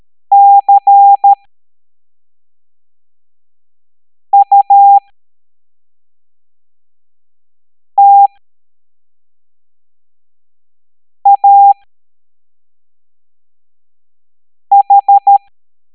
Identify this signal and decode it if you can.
Morse code, heard as a keyed tone, about 13 wpm, spaced as for 3 wpm, 800 Hz. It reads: CUTAH
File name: hamdata / L09-EX04